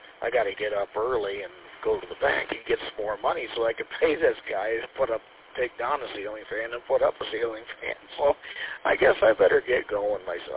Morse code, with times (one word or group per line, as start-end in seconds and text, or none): none